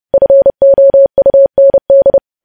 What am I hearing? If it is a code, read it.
Morse code, heard as a keyed tone, about 30 wpm, 550 Hz. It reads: FOUND
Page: news headlines in morse code at 30 wpm